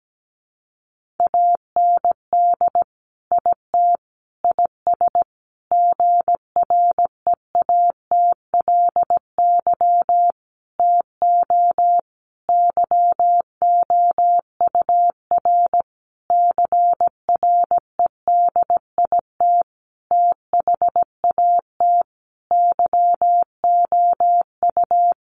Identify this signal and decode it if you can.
Morse code, heard as a keyed tone, about 17 wpm, 700 Hz. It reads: AND IT IS GREATLY TO YOUR CREDIT THAT YOU
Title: Sentences from Top 1400 Words 17wpm